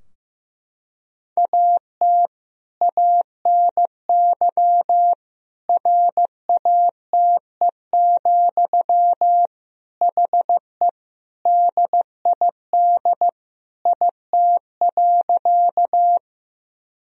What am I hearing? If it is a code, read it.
Morse code, heard as a keyed tone, about 15 wpm, 700 Hz. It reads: AT ANY RATE, HE DID IT.
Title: Sentences from Top 1000 Words - Review Part 4 - 15wpm